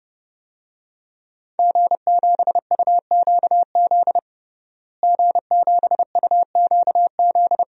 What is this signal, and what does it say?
Morse code, heard as a keyed tone, about 32 wpm, 700 Hz. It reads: G7UQZ G7UQZ
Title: DXCC-100 Call Signs - 5 Characters - Repeated 2x - 30wpm